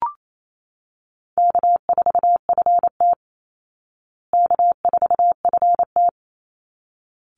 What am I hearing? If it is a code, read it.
Morse code, heard as a keyed tone, about 28 wpm, 700 Hz. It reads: K4FT K4FT
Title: Call Signs Repeated 3x - 28wpm